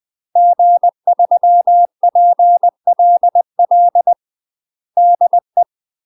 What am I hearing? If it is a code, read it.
Morse code, heard as a keyed tone, about 20 wpm, 700 Hz. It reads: G3PLL DE
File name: UK MASTER QSOs TRAINING/20WPM